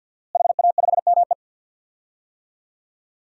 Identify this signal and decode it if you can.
Morse code, heard as a keyed tone, about 49 wpm, 700 Hz. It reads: HA5ME
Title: DXCC-100 Call Signs - 5 Characters - Repeated 3x - 50wpm